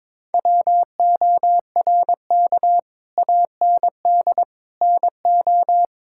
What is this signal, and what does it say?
Morse code, heard as a keyed tone, about 22 wpm, 700 Hz. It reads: WORK AND NO
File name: practice / 22wpm / Sets of 8 Words - 22wpm